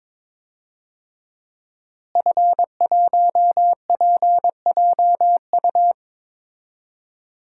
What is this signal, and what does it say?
Morse code, heard as a keyed tone, about 22 wpm, 700 Hz. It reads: F1PJU
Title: DXCC-100 Call Signs - 5 Characters - Repeated 2x - 22wpm